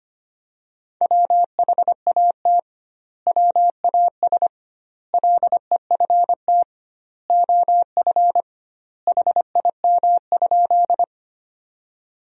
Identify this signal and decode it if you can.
Morse code, heard as a keyed tone, about 25 wpm, 700 Hz. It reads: WHAT WAS LEFT OF HIM?
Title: 2x Word-Spacing - Sentences from Top 300 Words - 25wpm